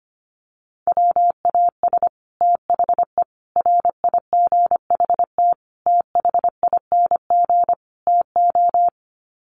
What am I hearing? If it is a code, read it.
Morse code, heard as a keyed tone, about 25 wpm, 700 Hz. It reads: WAS THE RIGHT THING TO
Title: Sets of 5 Words Encore2 25wpm